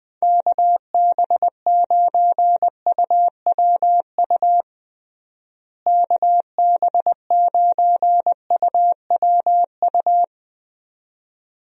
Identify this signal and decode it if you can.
Morse code, heard as a keyed tone, about 20 wpm, 700 Hz. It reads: KB9UWU KB9UWU
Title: USA Call Signs - 6 Characters - Repeated 2x - 20wpm